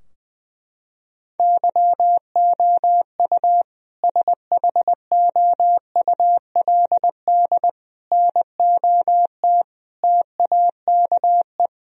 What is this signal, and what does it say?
Morse code, heard as a keyed tone, about 20 wpm, 700 Hz. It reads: YOU SHOULD NOT TAKE
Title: Sentences from Top 1000 Words - Review Part 3 - 20wpm